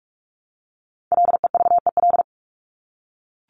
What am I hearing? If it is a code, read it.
Morse code, heard as a keyed tone, about 44 wpm, 700 Hz. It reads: LEVEL
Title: Top 501-750 Words 45wpm